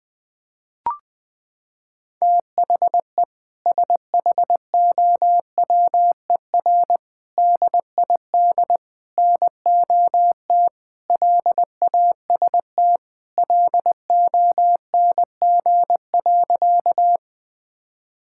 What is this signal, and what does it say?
Morse code, heard as a keyed tone, about 21 wpm, 700 Hz. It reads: THE SHOWER DID NOT LAST LONG.
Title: Sentences from Top 1600 Words 20wpm